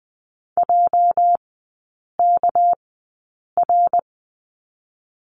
Morse code, heard as a keyed tone, about 20 wpm, spaced as for 8 wpm, 690 Hz. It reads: JKR
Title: 212 Lesson 34e - Number 8 - Three Characters 20wpm